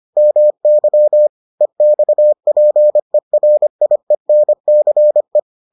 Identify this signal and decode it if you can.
Morse code, heard as a keyed tone, about 25 wpm, 600 Hz. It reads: MY EXPERIENCE